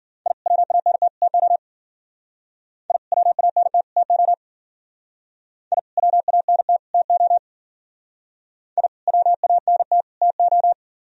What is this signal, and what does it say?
Morse code, heard as a keyed tone, about 48 wpm, 700 Hz. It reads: I WANT TO I WANT TO I WANT TO I WANT TO
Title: Speed-Racing - Sets of 3 Words 40wpm